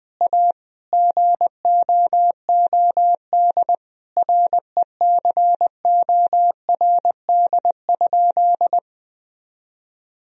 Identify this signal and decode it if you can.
Morse code, heard as a keyed tone, about 21 wpm, 700 Hz. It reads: A GOOD RECORD?
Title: Sentences from Top 1000 Words - Review Part 3 - 20wpm